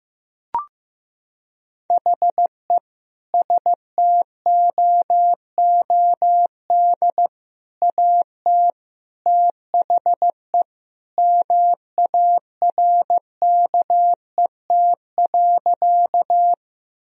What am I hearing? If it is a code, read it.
Morse code, heard as a keyed tone, about 15 wpm, 700 Hz. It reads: HE STOOD AT THE MARKET.
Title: Sentences from Top 1000 Words - Review Part 3 - 15wpm